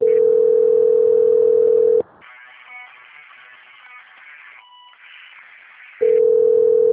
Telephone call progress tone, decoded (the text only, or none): ringback tone